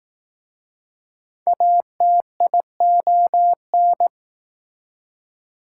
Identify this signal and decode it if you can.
Morse code, heard as a keyed tone, about 19 wpm, 700 Hz. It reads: ATION